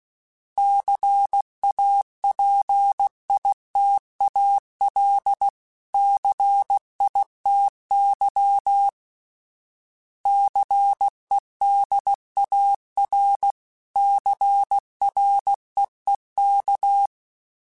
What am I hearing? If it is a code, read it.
Morse code, heard as a keyed tone, about 16 wpm, 800 Hz. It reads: CAPITAL CITY CEDAR CREEK
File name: fromSMs / FromSMs mp3s / Cities@15